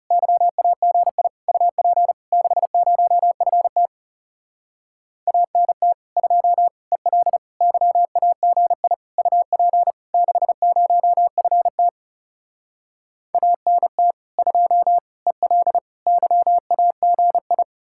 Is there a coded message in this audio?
Yes: Morse code from 0.1 to 17.6 s, heard as a keyed tone, about 40 wpm, 700 Hz. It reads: YAGI UP 60FT ANT 2 EL YAGI UP 60FT ANT 2 EL YAGI